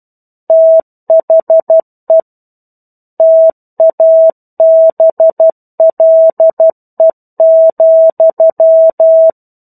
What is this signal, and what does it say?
Morse code, heard as a keyed tone, about 12 wpm, 650 Hz. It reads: THE TABLE,
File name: 01-Chapter-01 12 WPM